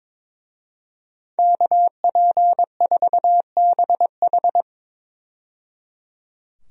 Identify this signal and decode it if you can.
Morse code, heard as a keyed tone, about 22 wpm, 700 Hz. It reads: KP4BH
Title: DXCC-100 Call Signs - 5 Characters - 22wpm